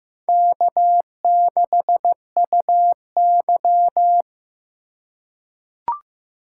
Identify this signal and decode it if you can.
Morse code, heard as a keyed tone, about 15 wpm, 700 Hz. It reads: K6UY
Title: USA Call Signs - 4 Characters - 4x Character Spacing - 15wpm